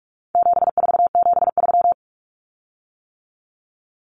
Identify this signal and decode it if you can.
Morse code, heard as a keyed tone, about 44 wpm, 700 Hz. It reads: 7473
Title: Common Cut Numbers - 4-Digit 45wpm